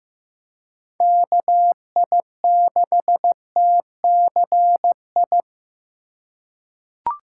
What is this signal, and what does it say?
Morse code, heard as a keyed tone, about 15 wpm, 700 Hz. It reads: KI6TCI
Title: USA Call Signs - 6 Characters - 2x Character Spacing - 15wpm